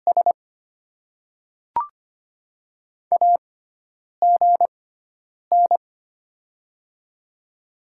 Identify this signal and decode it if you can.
Morse code, heard as a keyed tone, about 25 wpm, spaced as for 9 wpm, 700 Hz. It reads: S AGN
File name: Top 20 QSO Elements - 6x Character Spacing - 25wpm